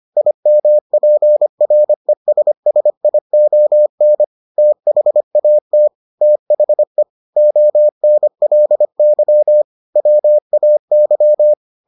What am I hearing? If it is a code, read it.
Morse code, heard as a keyed tone, about 25 wpm, 600 Hz. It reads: IMPRESSION THAT THE ONLY WAY